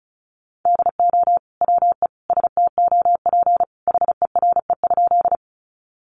Morse code, heard as a keyed tone, about 34 wpm, 700 Hz. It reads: DO WE STOP HERE?
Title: Sentences from Top 300 Words 35wpm